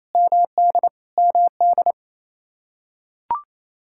Morse code, heard as a keyed tone, about 27 wpm, 700 Hz. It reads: MD MD
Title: K1USN SST QSO Practice 28wpm